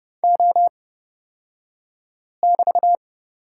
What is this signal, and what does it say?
Morse code, heard as a keyed tone, about 29 wpm, spaced as for 5 wpm, 690 Hz. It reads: O<BT>
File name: Lesson 36d - BT - Two Characters 30wpm